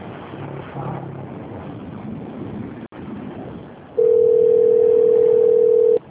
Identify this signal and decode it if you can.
Ringback tone (North America), 440 + 480 Hz.